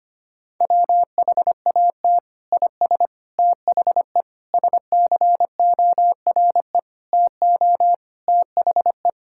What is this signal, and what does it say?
Morse code, heard as a keyed tone, about 25 wpm, 700 Hz. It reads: WHAT IS THE SCORE TO THE
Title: Sentences from Top 1600 Words 25wpm